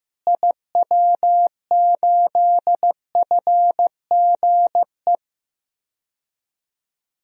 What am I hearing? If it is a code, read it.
Morse code, heard as a keyed tone, about 15 wpm, 700 Hz. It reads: IW8FGE
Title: DXCC-100 Call Signs - 6 Characters - Repeated 2x - 15wpm